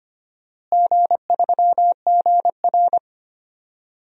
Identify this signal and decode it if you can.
Morse code, heard as a keyed tone, about 25 wpm, 700 Hz. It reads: G3GR
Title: DXCC-100 Call Signs - 4 Characters - Repeated 2x - 25wpm